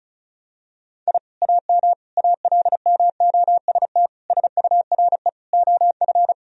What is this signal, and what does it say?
Morse code, heard as a keyed tone, about 34 wpm, 700 Hz. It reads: I AM ALMOST SURE OF
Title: Sentences from Top 1000 Words - Review Part 4 - 35wpm